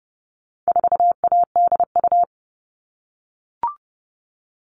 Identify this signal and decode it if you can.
Morse code, heard as a keyed tone, about 30 wpm, 700 Hz. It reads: 4ADU
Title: All Cut Numbers - 4-Digit 30wpm